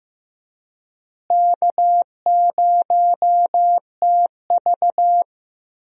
Morse code, heard as a keyed tone, about 15 wpm, 700 Hz. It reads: K0TV
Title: USA Call Signs - 4 Characters - Repeated 3x - 15wpm